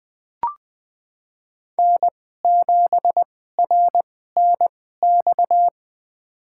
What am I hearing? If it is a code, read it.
Morse code, heard as a keyed tone, about 21 wpm, spaced as for 15 wpm, 700 Hz. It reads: N7RNX
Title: USA Call Signs - 5 Characters - 2x Character Spacing - 20wpm